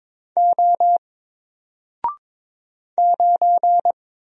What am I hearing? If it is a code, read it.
Morse code, heard as a keyed tone, about 22 wpm, 700 Hz. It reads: O 9
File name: Single-Letter-Number-22wpm